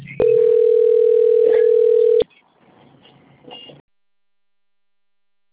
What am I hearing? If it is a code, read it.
Ringback tone (North America), 440 + 480 Hz.